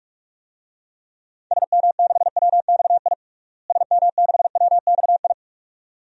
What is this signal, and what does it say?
Morse code, heard as a keyed tone, about 42 wpm, 700 Hz. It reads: SM6WXI SM6WXI
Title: DXCC-100 Call Signs - 6 Characters - Repeated 2x - 45wpm